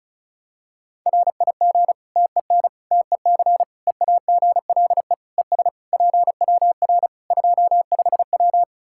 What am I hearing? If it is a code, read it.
Morse code, heard as a keyed tone, about 36 wpm, 700 Hz. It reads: RIG TEN TEC EAGLE ES PWR 25W
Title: K1USN SST QSO Practice 35wpm